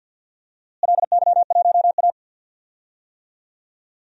Morse code, heard as a keyed tone, about 46 wpm, 700 Hz. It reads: LY1A
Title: DXCC-100 Call Signs - 4 Characters - Repeated 3x - 50wpm